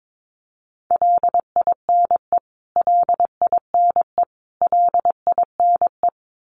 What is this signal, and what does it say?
Morse code, heard as a keyed tone, about 22 wpm, 700 Hz. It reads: LINE LINE LINE
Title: Top 300 Words Repeated 22wpm